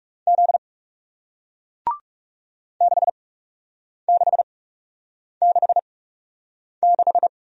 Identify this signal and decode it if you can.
Morse code, heard as a keyed tone, about 40 wpm, spaced as for 8 wpm, 700 Hz. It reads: Z 6666